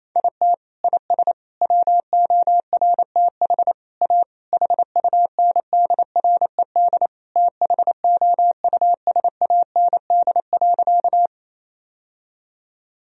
Morse code, heard as a keyed tone, about 28 wpm, 700 Hz. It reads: IT IS WORTH A HUNDRED THOUSAND.
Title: Sentences from Top 1000 Words - Review Part 3 - 28wpm